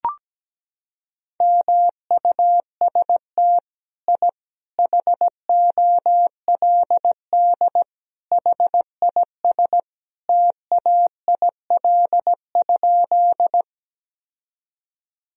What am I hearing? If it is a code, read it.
Morse code, heard as a keyed tone, about 17 wpm, 700 Hz. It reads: MUST I HOLD HIS TAIL?
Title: Sentences from Top 1000 Words - Review Part 3 - 17wpm